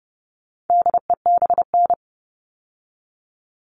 Morse code, heard as a keyed tone, about 30 wpm, 700 Hz. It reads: DEBN
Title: All Cut Numbers - 4-Digit 30wpm